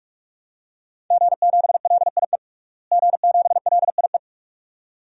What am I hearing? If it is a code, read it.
Morse code, heard as a keyed tone, about 46 wpm, 700 Hz. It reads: G7LIE G7LIE
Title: DXCC-100 Call Signs - 5 Characters - Repeated 2x - 45wpm